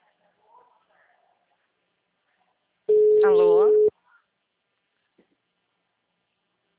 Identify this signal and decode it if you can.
Ringback tone (Europe, 425 Hz), with speech.
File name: out- -2019- - - .46